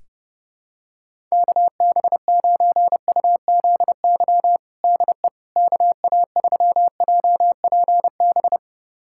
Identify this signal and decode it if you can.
Morse code, heard as a keyed tone, about 32 wpm, 700 Hz. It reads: KB9UZY DE KA3JPB
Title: USA Callsign DE Callsign - 6 Characters - 30wpm